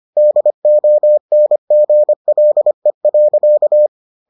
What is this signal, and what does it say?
Morse code, heard as a keyed tone, about 25 wpm, 600 Hz. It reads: DONGLE.